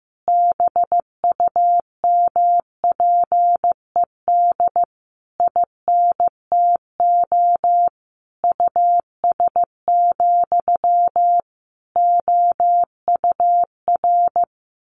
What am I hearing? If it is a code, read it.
Morse code, heard as a keyed tone, about 15 wpm, 700 Hz. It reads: BUMPED INTO US, OUR
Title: A Sound Of Thunder 15wpm